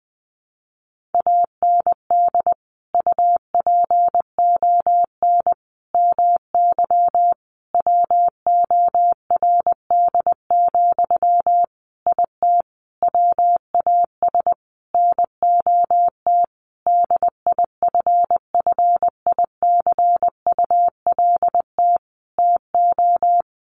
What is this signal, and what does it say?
Morse code, heard as a keyed tone, about 20 wpm, 700 Hz. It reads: AND UPON MY WORD, IT WAS NOT DIFFICULT TO
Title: Sentences from Top 1100 Words 20wpm